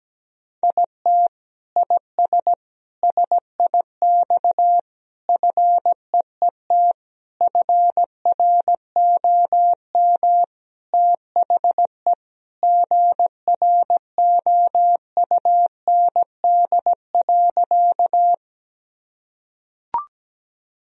Morse code, heard as a keyed tone, about 17 wpm, 700 Hz. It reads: IT IS SIX FEET FROM THE GROUND.
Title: Sentences from Top 1000 Words - Review Part 2 - 17wpm